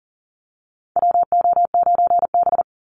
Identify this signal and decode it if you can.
Morse code, heard as a keyed tone, about 40 wpm, 700 Hz. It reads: WO9B
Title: Call Signs Repeated 3x - 40wpm